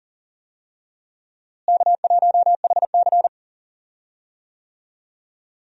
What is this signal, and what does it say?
Morse code, heard as a keyed tone, about 40 wpm, 700 Hz. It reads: K1HC
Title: USA Call Signs - 4 Characters - Repeated 2x - 40wpm